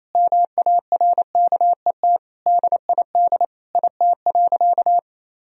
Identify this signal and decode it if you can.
Morse code, heard as a keyed tone, about 28 wpm, 700 Hz. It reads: MARKET DID IT.
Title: Sentences from Top 1000 Words - Review Part 3 - 28wpm